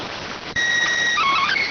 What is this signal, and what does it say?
SSTV (slow-scan television), PD 290.